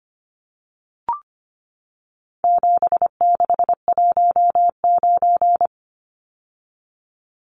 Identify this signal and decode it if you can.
Morse code, heard as a keyed tone, about 25 wpm, 700 Hz. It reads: 7619
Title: Numbers - 4-Digit 25wpm